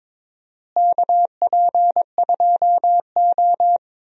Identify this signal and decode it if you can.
Morse code, heard as a keyed tone, about 22 wpm, 700 Hz. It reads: KP2O
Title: DXCC-100 Call Signs - 4 Characters - Repeated 3x - 22wpm